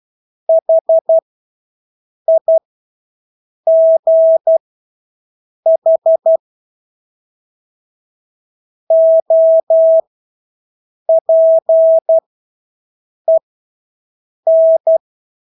Morse code, heard as a keyed tone, about 12 wpm, spaced as for 6 wpm, 650 Hz. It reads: HIGH OPEN
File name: Engelske ord 12-6 wpm